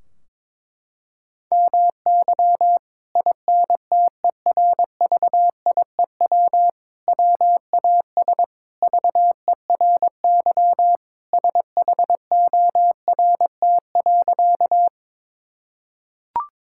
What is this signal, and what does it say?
Morse code, heard as a keyed tone, about 22 wpm, 700 Hz. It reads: MY INTERVIEW WAS VERY SHORT.
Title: Sentences from Top 1600 Words 22wpm